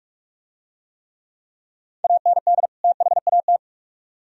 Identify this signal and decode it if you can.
Morse code, heard as a keyed tone, about 41 wpm, 700 Hz. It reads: AND THAT